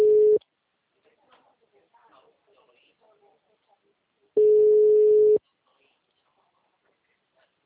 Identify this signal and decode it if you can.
Ringback tone (Europe, 425 Hz), with speech.